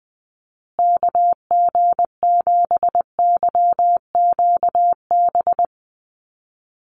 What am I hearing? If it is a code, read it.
Morse code, heard as a keyed tone, about 20 wpm, 700 Hz. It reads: KG7YQB
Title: USA Call Signs Repeated 2x 20wpm